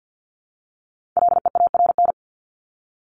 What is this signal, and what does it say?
Morse code, heard as a keyed tone, about 50 wpm, 700 Hz. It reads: LEARN
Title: Top 300 Words 50wpm